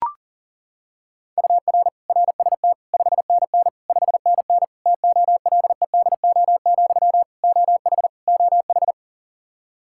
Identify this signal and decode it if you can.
Morse code, heard as a keyed tone, about 40 wpm, 700 Hz. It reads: UR RST 5NN 5NN TOLEDO, OH OH
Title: Practice QSO - Exchange 1 - Beginning - 40wpm